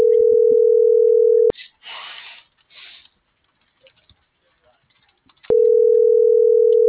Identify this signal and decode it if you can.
Ringback tone (North America), 440 + 480 Hz.